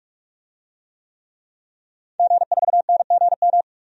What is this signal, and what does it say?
Morse code, heard as a keyed tone, about 41 wpm, 700 Hz. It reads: G4NGM
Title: DXCC-100 Call Signs - 5 Characters - 45wpm